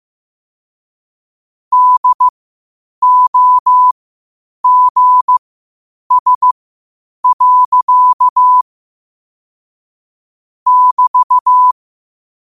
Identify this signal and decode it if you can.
Morse code, heard as a keyed tone, about 15 wpm, spaced as for 8 wpm, 1000 Hz. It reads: DOGS. <BT>
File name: qotdfiles / 8wpm